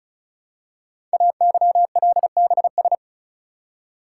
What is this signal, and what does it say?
Morse code, heard as a keyed tone, about 36 wpm, 700 Hz. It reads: AYLBS
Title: Code Groups - 5 Letter:Numbers - 35wpm